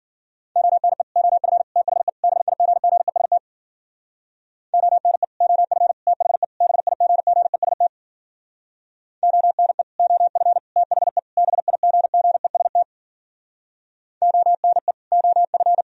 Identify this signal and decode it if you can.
Morse code, heard as a keyed tone, about 50 wpm, 700 Hz. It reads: ONE OF THE BIGGEST ONE OF THE BIGGEST ONE OF THE BIGGEST ONE OF